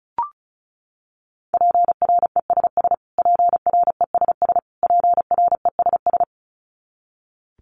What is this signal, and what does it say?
Morse code, heard as a keyed tone, about 34 wpm, 700 Hz. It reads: PRESS PRESS PRESS
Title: Top 300 Words Repeated 35wpm